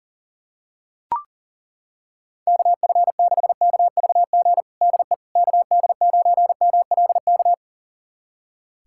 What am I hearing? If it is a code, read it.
Morse code, heard as a keyed tone, about 40 wpm, 700 Hz. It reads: KF6KVG DE KD9MLK
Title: USA Callsign DE Callsign 40wpm